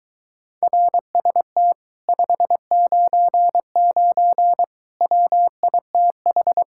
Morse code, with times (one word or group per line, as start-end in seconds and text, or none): 0.6-1.7 RST
2.1-4.6 599
5.0-6.6 WITH